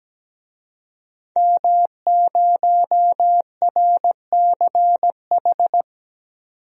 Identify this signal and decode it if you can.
Morse code, heard as a keyed tone, about 17 wpm, 700 Hz. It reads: M0RCH
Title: DXCC-100 Call Signs - 5 Characters - 17wpm